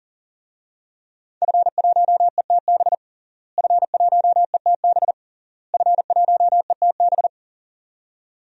Morse code, heard as a keyed tone, about 40 wpm, 700 Hz. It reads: F1ETB F1ETB F1ETB
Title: DXCC-100 Call Signs - 5 Characters - Repeated 3x - 40wpm